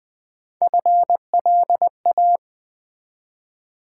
Morse code, heard as a keyed tone, about 20 wpm, 700 Hz. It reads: FLA
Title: Build A Word - 1 Letter At A Time - 5-Letter Words - 20wpm